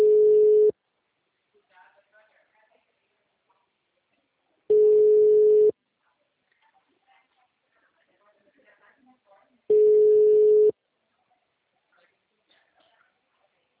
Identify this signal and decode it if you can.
Ringback tone (Europe, 425 Hz), with speech.
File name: out- -2017- - - .25